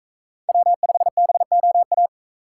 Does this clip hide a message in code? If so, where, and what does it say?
Yes: Morse code from 0.5 to 2.1 s, heard as a keyed tone, about 40 wpm, 700 Hz. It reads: W5BOA